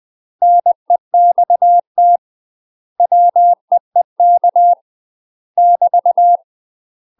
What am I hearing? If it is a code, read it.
Morse code, heard as a keyed tone, about 20 wpm, 700 Hz. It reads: NEXT WEEK <BT>
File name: UK MASTER QSOs TRAINING / 20WPM / UK MASTER QSOs 0011